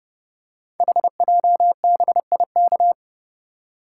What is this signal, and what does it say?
Morse code, heard as a keyed tone, about 30 wpm, 700 Hz. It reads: HJBIK
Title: Code Groups - 5 Letters - 30wpm